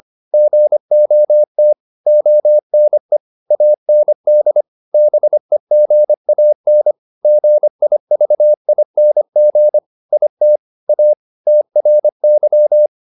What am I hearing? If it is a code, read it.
Morse code, heard as a keyed tone, about 25 wpm, 600 Hz. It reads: GOT ONE AND BEGAN GIVING IT A TRY